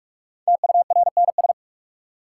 Morse code, heard as a keyed tone, about 46 wpm, 700 Hz. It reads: TURNS
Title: Sets of 5 Words Encore - New Word Drill - 45wpm